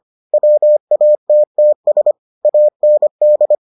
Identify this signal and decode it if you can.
Morse code, heard as a keyed tone, about 25 wpm, 600 Hz. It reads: WATTS AND